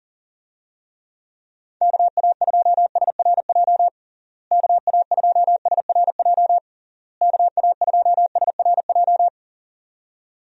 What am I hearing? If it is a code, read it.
Morse code, heard as a keyed tone, about 40 wpm, 700 Hz. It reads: KA2SRJ KA2SRJ KA2SRJ
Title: USA Call Signs - 6 Characters - Repeated 3x - 40wpm